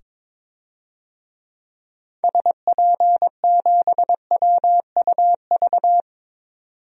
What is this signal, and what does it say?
Morse code, heard as a keyed tone, about 22 wpm, 700 Hz. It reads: SP7WUV
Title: DXCC-100 Call Signs - 6 Characters - 22wpm